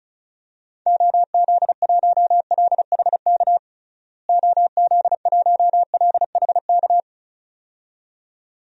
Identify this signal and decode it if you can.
Morse code, heard as a keyed tone, about 34 wpm, 700 Hz. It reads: OZ1LHK OZ1LHK
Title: DXCC-100 Call Signs - 6 Characters - Repeated 2x - 35wpm